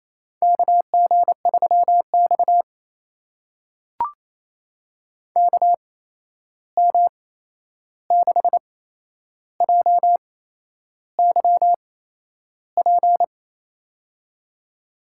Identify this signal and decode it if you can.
Morse code, heard as a keyed tone, about 28 wpm, spaced as for 11 wpm, 700 Hz. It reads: KG3X KM6JYP
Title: USA Call Signs - 8x Character Spacing - 28wpm